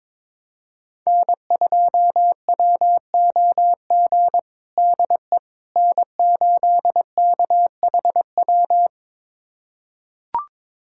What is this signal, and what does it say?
Morse code, heard as a keyed tone, about 22 wpm, 700 Hz. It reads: N2WOG DE N8KHW
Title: USA Callsign DE Callsign - 5 Characters - 22wpm